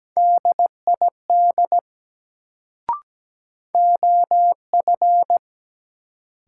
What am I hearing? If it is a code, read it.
Morse code, heard as a keyed tone, about 17 wpm, 700 Hz. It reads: DID OF
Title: Sentences from Top 100 Words - New Word Drill - 17wpm